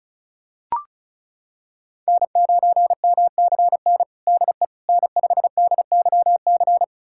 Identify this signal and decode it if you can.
Morse code, heard as a keyed tone, about 34 wpm, 700 Hz. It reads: N9MCN DE N5DYC
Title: USA Callsign DE Callsign - 5 Characters - 35wpm